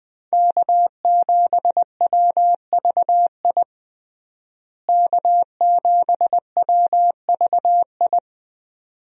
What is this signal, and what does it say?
Morse code, heard as a keyed tone, about 20 wpm, 700 Hz. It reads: K7WVI K7WVI
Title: USA Call Signs - 5 Characters - Repeated 2x - 20wpm